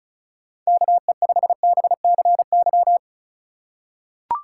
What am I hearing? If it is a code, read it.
Morse code, heard as a keyed tone, about 34 wpm, 700 Hz. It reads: KE5BCY
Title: K1USN SST QSO Practice - 35wpm